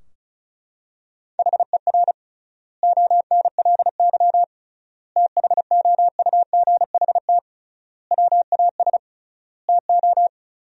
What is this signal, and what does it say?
Morse code, heard as a keyed tone, about 36 wpm, 700 Hz. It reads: HER ONLY THOUGHT WAS TO
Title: 3x Word-Spacing - Sentences from Top 400 Words - 35wpm